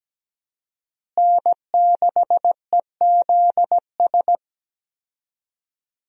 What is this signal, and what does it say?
Morse code, heard as a keyed tone, about 17 wpm, 700 Hz. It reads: N6EZS
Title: USA Call Signs - 5 Characters - Repeated 2x - 17wpm